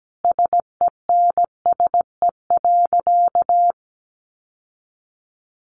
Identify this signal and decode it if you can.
Morse code, heard as a keyed tone, about 17 wpm, 700 Hz. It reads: SENSE.
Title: Sentences from Top 900 Words 17wpm